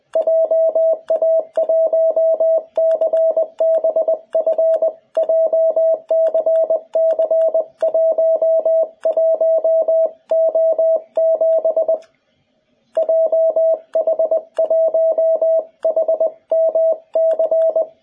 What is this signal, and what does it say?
Morse code, heard as a keyed tone, about 20 wpm, 650 Hz. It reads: JA1CBF JCC11O7 JH1HMC
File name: jh8cbh / 2021cw